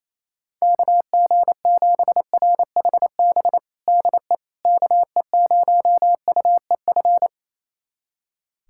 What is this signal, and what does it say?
Morse code, heard as a keyed tone, about 28 wpm, 700 Hz. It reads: KG7RHB DE KE0UEF
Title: USA Callsign DE Callsign - 6 Characters - 28wpm